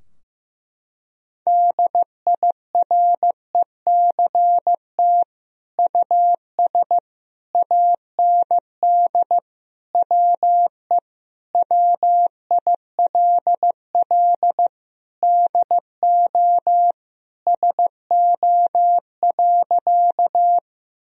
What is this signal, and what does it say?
Morse code, heard as a keyed tone, about 14 wpm, 700 Hz. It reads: DIRECT US AND WE WILL DO SO.
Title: Sentences from Top 1000 Words - Review Part 2 - 15wpm